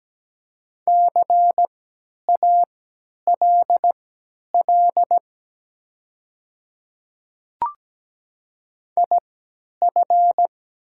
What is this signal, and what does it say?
Morse code, heard as a keyed tone, about 17 wpm, spaced as for 10 wpm, 700 Hz. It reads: CALL IF
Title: 3x Character Spacing - Top 100 Words - 17wpm